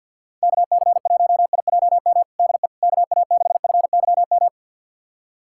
Morse code, heard as a keyed tone, about 49 wpm, 700 Hz. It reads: KC1IJM DE KA6FYM
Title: USA Callsign DE Callsign - 6 Characters - 50wpm